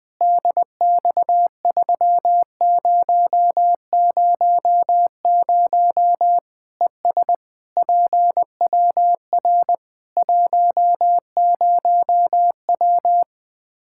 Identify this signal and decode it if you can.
Morse code, heard as a keyed tone, about 21 wpm, 700 Hz. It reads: DX3000 ES PWR 10W
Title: Practice QSO - Exchange 2 - Info - 20wpm